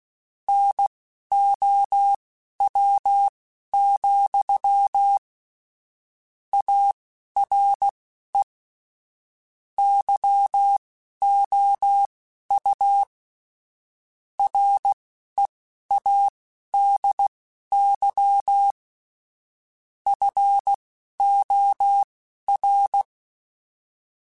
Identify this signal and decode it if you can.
Morse code, heard as a keyed tone, about 16 wpm, spaced as for 11 wpm, 800 Hz. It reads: NOW, ARE YOU READY FOR
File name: Technical Subjects / Technical Subjects-mp3s / OR Gate @10